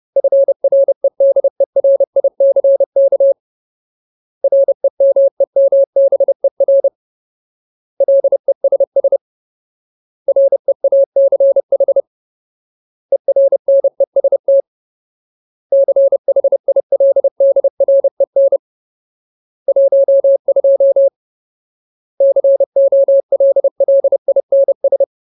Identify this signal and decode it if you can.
Morse code, heard as a keyed tone, about 30 wpm, 550 Hz. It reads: FREDERICK REMEMBER LESS REACH ERNEST CHILDREN 12 COLLINS
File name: Ultimate Word List / 30WPM Ultimate